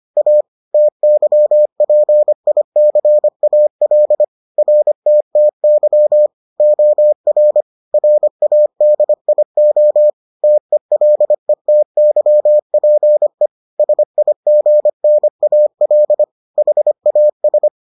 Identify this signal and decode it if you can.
Morse code, heard as a keyed tone, about 25 wpm, 600 Hz. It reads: A TYPICAL RTTY OR RADIO TELETYPE SIGNAL HAS